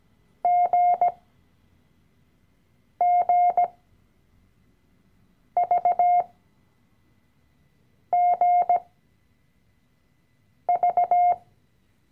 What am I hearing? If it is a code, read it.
Morse code, heard as a keyed tone, about 17 wpm, spaced as for 4 wpm, 710 Hz. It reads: GGVGV
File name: CW-Online-Kurs / Anfänger / Lektion 012